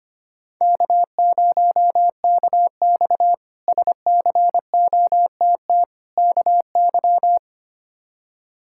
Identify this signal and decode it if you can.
Morse code, heard as a keyed tone, about 25 wpm, 700 Hz. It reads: K0KX SCOTT KY
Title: K1USN SST QSO Practice - 25wpm